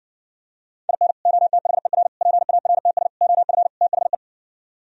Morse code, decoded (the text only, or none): IN OTHER PARTS OF THE